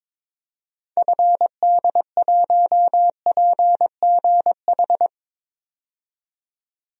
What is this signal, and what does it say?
Morse code, heard as a keyed tone, about 22 wpm, 700 Hz. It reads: FD1PGH
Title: DXCC-100 Call Signs - 6 Characters - 22wpm